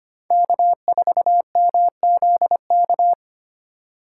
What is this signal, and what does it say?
Morse code, heard as a keyed tone, about 25 wpm, 700 Hz. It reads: K4MZK
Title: USA Call Signs - 5 Characters - 3x Character Spacing - 25wpm